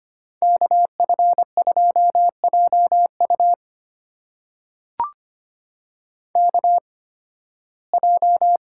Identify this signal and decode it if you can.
Morse code, heard as a keyed tone, about 25 wpm, spaced as for 16 wpm, 700 Hz. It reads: KF2JU KJ
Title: USA Call Signs - 5 Characters - 8x Character Spacing - 25wpm